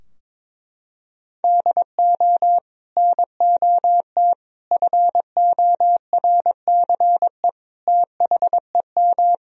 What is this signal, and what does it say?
Morse code, heard as a keyed tone, about 22 wpm, 700 Hz. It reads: DO NOT FORCE THEM
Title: Sentences from Top 1000 Words - Review Part 3 - 22wpm